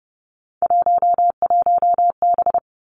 Morse code, heard as a keyed tone, about 30 wpm, 700 Hz. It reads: 11B
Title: Three Challenging Letter-Number Combinations 30wpm